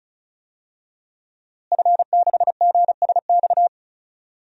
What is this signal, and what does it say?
Morse code, heard as a keyed tone, about 34 wpm, 700 Hz. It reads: F6GSX
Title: DXCC-100 Call Signs - 5 Characters - Repeated 2x - 35wpm